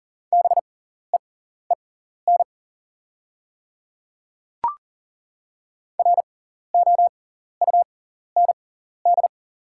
Morse code, heard as a keyed tone, about 40 wpm, spaced as for 14 wpm, 690 Hz. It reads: BEEN ROUND
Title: 6x Character Spacing - Top 200 Words - 40wpm